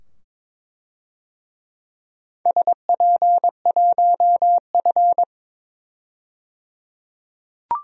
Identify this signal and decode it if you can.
Morse code, heard as a keyed tone, about 22 wpm, 700 Hz. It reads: SP1F
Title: DXCC-100 Call Signs - 4 Characters - 22wpm